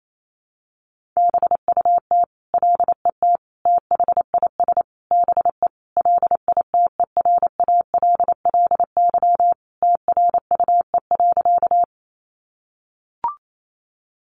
Morse code, heard as a keyed tone, about 28 wpm, 700 Hz. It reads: BUT LET THIS BE LITERALLY TRUE.
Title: Sentences from Top 1400 Words 28wpm